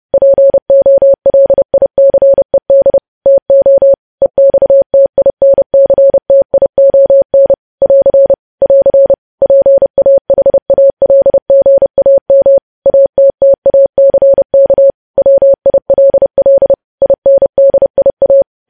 Morse code, heard as a keyed tone, about 30 wpm, 550 Hz. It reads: POLICED TO EXTINCTION <AR> <AR> PAHALGAM ATTACK WILL INDIA
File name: podcast / News headlines-1x-30x30